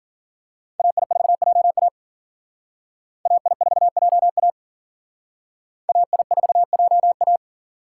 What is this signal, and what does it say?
Morse code, heard as a keyed tone, about 47 wpm, 700 Hz. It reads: AI4JA AI4JA AI4JA
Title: Speed Racing - USA 2x2 Call Signs 40wpm